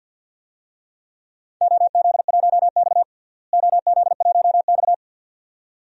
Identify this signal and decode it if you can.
Morse code, heard as a keyed tone, about 50 wpm, 700 Hz. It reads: OZ1X OZ1X
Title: DXCC-100 Call Signs - 4 Characters - Repeated 2x - 50wpm